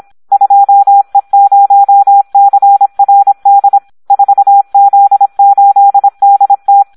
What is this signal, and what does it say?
Morse code, heard as a keyed tone, about 26 wpm, 800 Hz, with radio noise. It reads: 2E0CRD 4Z8DT
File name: Audio practica CW / SD 1 26